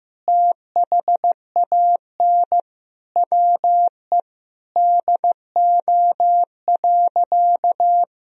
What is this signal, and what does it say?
Morse code, heard as a keyed tone, about 16 wpm, 700 Hz. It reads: THAN WE DO.
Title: Sentences from Top 100 Words and Abbreviations 15wpm